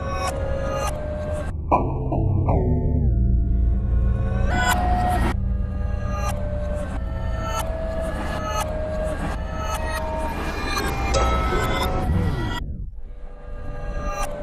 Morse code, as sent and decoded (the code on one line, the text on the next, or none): none